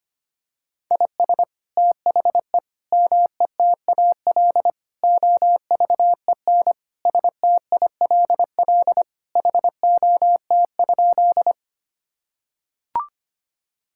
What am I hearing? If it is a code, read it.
Morse code, heard as a keyed tone, about 25 wpm, 700 Hz. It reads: IS THE METAL OVEN STILL HOT?
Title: Sentences from Top 1000 Words - Review Part 6 - 25wpm